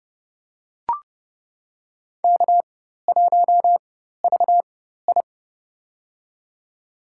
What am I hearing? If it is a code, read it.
Morse code, heard as a keyed tone, about 29 wpm, spaced as for 14 wpm, 700 Hz. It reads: K1VI
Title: USA Call Signs - 4 Characters - 4x Character Spacing - 30wpm